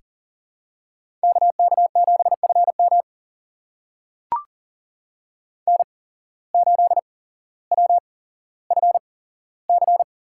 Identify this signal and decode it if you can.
Morse code, heard as a keyed tone, about 40 wpm, spaced as for 17 wpm, 700 Hz. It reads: KK7FM N8WFC
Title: USA Call Signs - 5 Characters - 8x Character Spacing - 40wpm